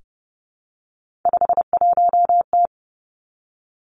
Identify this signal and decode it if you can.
Morse code, heard as a keyed tone, about 32 wpm, 700 Hz. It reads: 51T
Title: Three Challenging Letter-Number Combinations 30wpm